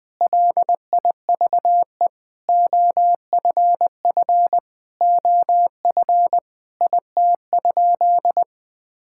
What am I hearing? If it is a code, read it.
Morse code, heard as a keyed tone, about 20 wpm, 700 Hz. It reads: LIVE OFF OF IT?
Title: Sentences from Top 1000 Words - Review Part 5 - 20wpm